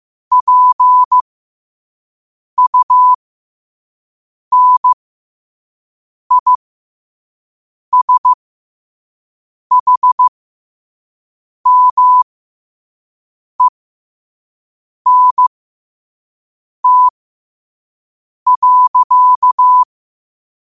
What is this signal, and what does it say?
Morse code, heard as a keyed tone, about 15 wpm, spaced as for 5 wpm, 1000 Hz. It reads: PUNISHMENT.
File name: qotdfiles / 5wpm